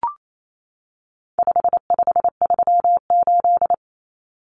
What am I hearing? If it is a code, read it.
Morse code, heard as a keyed tone, about 28 wpm, 700 Hz. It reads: 5538
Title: Common Cut Numbers - 4-Digit 28wpm